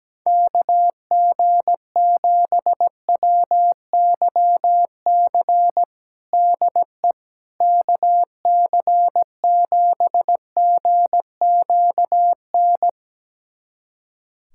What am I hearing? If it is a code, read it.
Morse code, heard as a keyed tone, about 17 wpm, 700 Hz. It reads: KG7WYC DE KC7GQN
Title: USA Callsign DE Callsign 17wpm